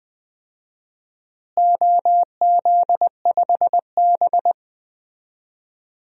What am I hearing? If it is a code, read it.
Morse code, heard as a keyed tone, about 20 wpm, 700 Hz. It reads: OZ5B
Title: DXCC-100 Call Signs - 4 Characters - Repeated 3x - 20wpm